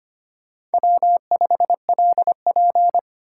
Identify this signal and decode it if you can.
Morse code, heard as a keyed tone, about 33 wpm, 700 Hz. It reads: W5LP